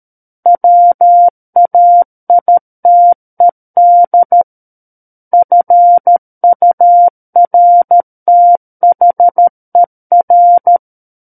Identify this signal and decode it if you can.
Morse code, heard as a keyed tone, about 13 wpm, 700 Hz. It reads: WAITED FURTHER
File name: Book One / 04-Chapter 4 - 13 WPM